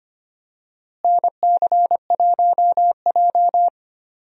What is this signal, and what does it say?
Morse code, heard as a keyed tone, about 25 wpm, 700 Hz. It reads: NC1J
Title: USA Call Signs - 4 Characters - Repeated 3x - 25wpm